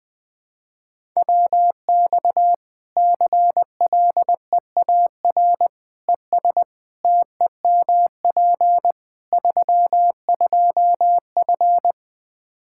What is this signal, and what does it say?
Morse code, heard as a keyed tone, about 21 wpm, 700 Hz. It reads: WX CLEAR ES TEMP 32F
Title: Practice QSO - Exchange 2 - Info - 20wpm